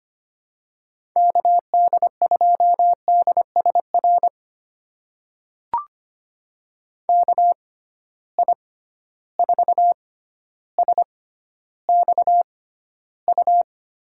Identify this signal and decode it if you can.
Morse code, heard as a keyed tone, about 25 wpm, spaced as for 13 wpm, 700 Hz. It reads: KD2DSR KI4SXU
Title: USA Call Signs - 6x Character Spacing - 25wpm